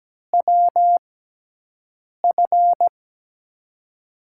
Morse code, heard as a keyed tone, about 17 wpm, spaced as for 6 wpm, 700 Hz. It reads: WF
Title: USA Call Signs - 6 Characters - 6x Character Spacing - 17wpm